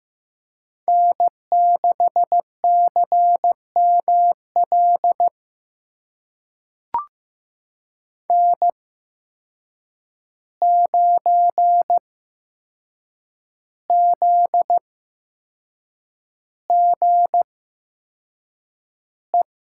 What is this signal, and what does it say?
Morse code, heard as a keyed tone, about 15 wpm, spaced as for 6 wpm, 700 Hz. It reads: N6CML N9ZGE
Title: USA Call Signs - 5 Characters - 8x Character Spacing - 15wpm